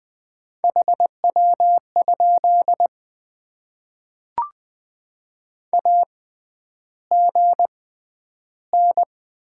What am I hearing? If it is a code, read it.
Morse code, heard as a keyed tone, about 20 wpm, 700 Hz. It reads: H W ? A G N
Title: Top 20 QSO Elements - 6x Character Spacing - 20wpm